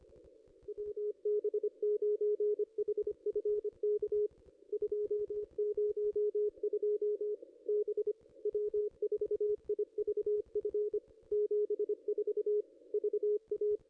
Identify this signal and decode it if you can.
Morse code, heard as a keyed tone, about 23 wpm, 410 Hz. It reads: WB9HFK 202 B W4IVF 74 VA
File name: contests / ss-cw-k5tr-op-n5rz -0200